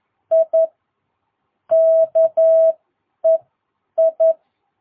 Morse code, heard as a keyed tone, about 11 wpm, spaced as for 8 wpm, 660 Hz. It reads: I KEI
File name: jh8cbh / 2022cw